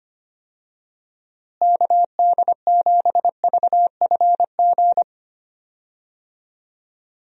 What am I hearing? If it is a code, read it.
Morse code, heard as a keyed tone, about 25 wpm, 700 Hz. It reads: KD7VFG
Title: USA Call Signs - 6 Characters - Repeated 2x - 25wpm